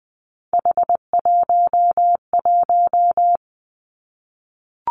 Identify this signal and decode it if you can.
Morse code, heard as a keyed tone, about 19 wpm, 700 Hz. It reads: H11
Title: Three Challenging Letter-Number Combinations 20wpm